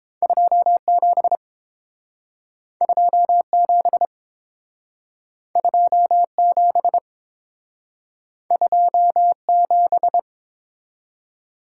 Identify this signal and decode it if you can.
Morse code, heard as a keyed tone, about 29 wpm, 700 Hz. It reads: 27 27 27 27